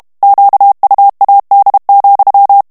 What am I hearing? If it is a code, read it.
Morse code, heard as a keyed tone, about 32 wpm, 800 Hz. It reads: QUAD,
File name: morse mp3 / HOLLANDAIS texte clair 1 vitesse 30 mots minute